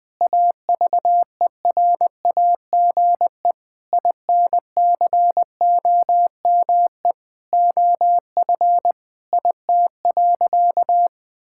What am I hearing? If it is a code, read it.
Morse code, heard as a keyed tone, about 21 wpm, 700 Hz. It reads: AVERAGE INCOME OF IT.
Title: Sentences from Top 1000 Words - Review Part 6 - 20wpm